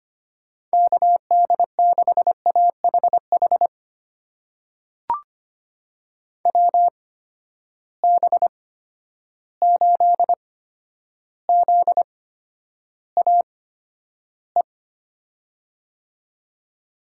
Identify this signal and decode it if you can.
Morse code, heard as a keyed tone, about 25 wpm, spaced as for 11 wpm, 700 Hz. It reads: KD6AHH WB8ZAE
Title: USA Call Signs - 6 Characters - 8x Character Spacing - 25wpm